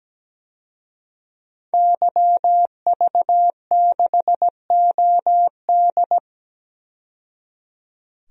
Morse code, heard as a keyed tone, about 17 wpm, 700 Hz. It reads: YV6OD
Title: DXCC-100 Call Signs - 5 Characters - 17wpm